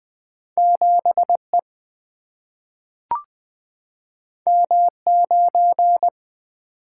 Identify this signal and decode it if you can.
Morse code, heard as a keyed tone, about 20 wpm, 700 Hz. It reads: 7E M9
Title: Single-Letter Single-Number 20wpm